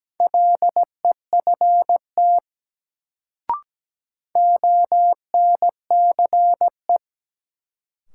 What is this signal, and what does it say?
Morse code, heard as a keyed tone, about 17 wpm, 700 Hz. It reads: LEFT ONCE
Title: Sentences from Top 300 Words - New Word Drill - 17wpm